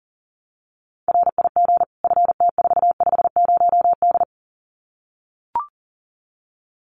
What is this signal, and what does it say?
Morse code, heard as a keyed tone, about 40 wpm, 700 Hz. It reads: RIG FT450D
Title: Common QSO Phrases 40wpm